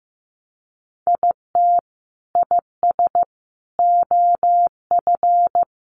Morse code, heard as a keyed tone, about 15 wpm, 700 Hz. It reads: IT IS OF
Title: Sentences from Top 1200 Words 15wpm